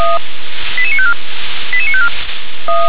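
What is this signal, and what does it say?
Five-tone selective call (ZVEI), heard as quick runs of five tones, tones of about 70 ms. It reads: 88954 88954
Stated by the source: Five-tone ZVEI selective call received on an SDR.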